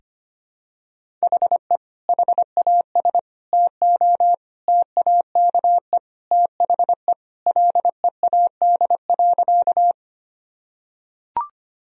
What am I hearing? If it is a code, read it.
Morse code, heard as a keyed tone, about 25 wpm, 700 Hz. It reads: HE HAS TO TAKE THE LEAD.
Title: Sentences from Top 1000 Words - Review Part 2 - 25wpm